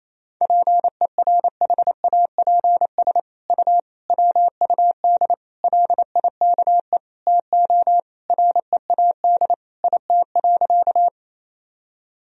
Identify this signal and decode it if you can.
Morse code, heard as a keyed tone, about 28 wpm, 700 Hz. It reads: PERHAPS U WUD LIKE TO READ IT.
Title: Sentences from Top 500 Words and Abbreviations 28wpm